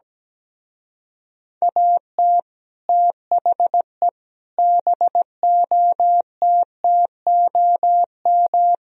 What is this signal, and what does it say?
Morse code, heard as a keyed tone, about 17 wpm, 700 Hz. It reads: AT THE BOTTOM
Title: Speed-Racing - Sets of 4 Words 17wpm